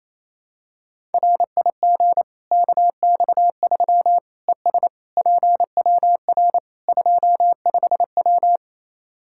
Morse code, heard as a keyed tone, about 28 wpm, 700 Hz. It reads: RIG KX3 ES PWR 25W
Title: Practice QSO - Exchange 2 - Info - 28wpm